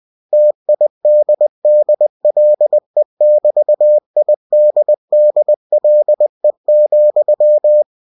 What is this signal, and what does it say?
Morse code, heard as a keyed tone, about 20 wpm, 600 Hz. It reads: TIDDLE<BT>IDDLE,
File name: cw-books / winnie-the-pooh / winnie-20wpm-page-0013